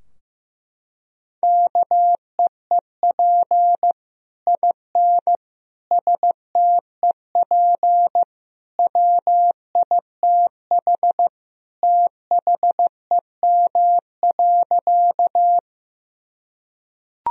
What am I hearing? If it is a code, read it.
Morse code, heard as a keyed tone, about 15 wpm, 700 Hz. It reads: KEEP IN STEP WITH THEM.
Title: Sentences from Top 1000 Words - Review Part 2 - 15wpm